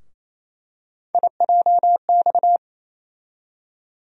Morse code, heard as a keyed tone, about 28 wpm, 700 Hz. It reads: IJX
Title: USA Call Sign Postfixes - 3 Letters 28wpm